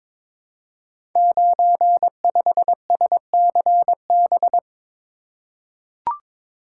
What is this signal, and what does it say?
Morse code, heard as a keyed tone, about 23 wpm, 700 Hz. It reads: 95SCB
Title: Code Groups - 5 Letter:Numbers - 22wpm